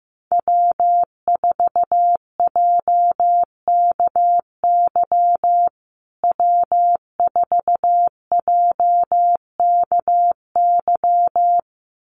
Morse code, heard as a keyed tone, about 15 wpm, 700 Hz. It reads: W4JKY W4JKY
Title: USA Call Signs Repeated 2x 15wpm